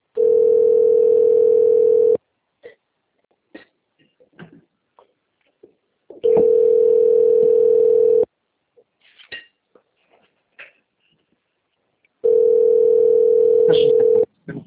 Ringback tone (North America), 440 + 480 Hz.